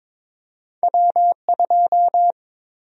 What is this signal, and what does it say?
Morse code, heard as a keyed tone, about 22 wpm, 700 Hz. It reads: W2
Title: USA Call Sign Prefixes - 22wpm